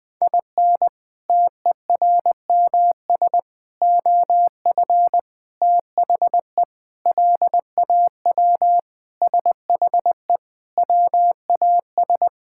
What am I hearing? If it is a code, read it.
Morse code, heard as a keyed tone, about 20 wpm, 700 Hz. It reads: IN TERMS OF THE LAW SHE WAS
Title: Sentences with 4-Word Phrases - 20wpm